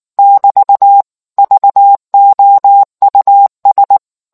Morse code, heard as a keyed tone, about 19 wpm, 800 Hz. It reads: <BT> VOUS